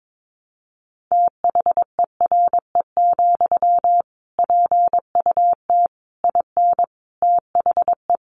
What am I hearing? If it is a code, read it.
Morse code, heard as a keyed tone, about 22 wpm, 700 Hz. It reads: THERE, PUT IN THE
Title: Sentences from Top 1200 Words 22wpm